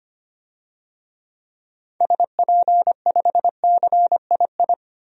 Morse code, heard as a keyed tone, about 25 wpm, 700 Hz. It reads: SP5CII
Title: DXCC-100 Call Signs - 6 Characters - 25wpm